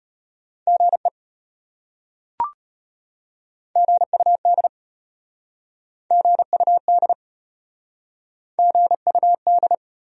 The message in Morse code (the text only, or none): GE GUD GUD GUD